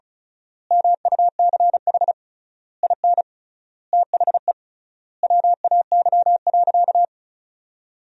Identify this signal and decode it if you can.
Morse code, heard as a keyed tone, about 36 wpm, 700 Hz. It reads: MUCH IN THE WAY.
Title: 3x Word-Spacing - Sentences from Top 200 Words - 35wpm